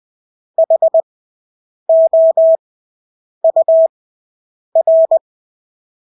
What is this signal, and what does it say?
Morse code, heard as a keyed tone, about 20 wpm, spaced as for 8 wpm, 650 Hz. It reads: HOUR
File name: Engelske ord 20/8 wpm